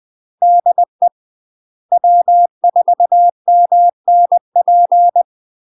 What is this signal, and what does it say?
Morse code, heard as a keyed tone, about 20 wpm, 700 Hz. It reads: DE W4MNP
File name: UK MASTER QSOs TRAINING / 20WPM / UK MASTER QSOs 0058